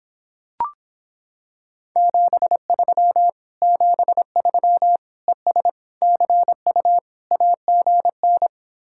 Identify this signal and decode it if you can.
Morse code, heard as a keyed tone, about 26 wpm, 700 Hz. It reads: 73 73 ES CU AGN